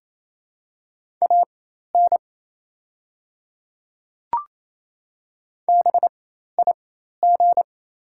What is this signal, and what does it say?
Morse code, heard as a keyed tone, about 28 wpm, spaced as for 13 wpm, 700 Hz. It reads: AN BIG
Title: 4x Character Spacing - Top 500 Words - 28wpm